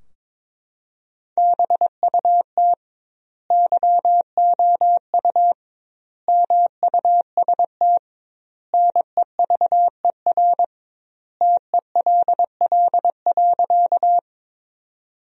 Morse code, heard as a keyed tone, about 21 wpm, 700 Hz. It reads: BUT YOU MUST NEVER TELL.
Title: 2x Word-Spacing - Sentences from Top 300 Words - 22wpm